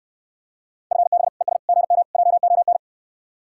Morse code, heard as a keyed tone, about 69 wpm, 700 Hz. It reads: FB ES GM JON